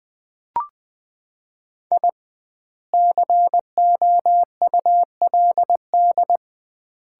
Morse code, heard as a keyed tone, about 20 wpm, 700 Hz. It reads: I COULD
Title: 2x Word-Spacing - Sets of 3 Words - 20wpm